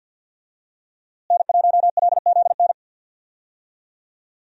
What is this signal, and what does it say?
Morse code, heard as a keyed tone, about 46 wpm, 700 Hz. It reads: N1LZN
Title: USA Call Signs - 5 Characters - Repeated 3x - 50wpm